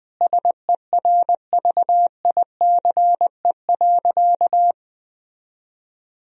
Morse code, heard as a keyed tone, about 21 wpm, 700 Hz. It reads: SERVICE.
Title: Sentences from Top 1000 Words - Review Part 4 - 20wpm